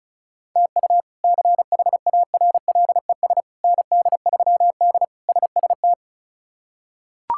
Morse code, heard as a keyed tone, about 36 wpm, 700 Hz. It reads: TU CHARLES ND3D SST
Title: K1USN SST QSO Practice - 35wpm